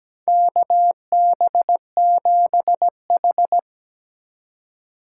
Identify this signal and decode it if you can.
Morse code, heard as a keyed tone, about 17 wpm, 700 Hz. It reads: KB7H
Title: K1USN SST QSO Practice - 17wpm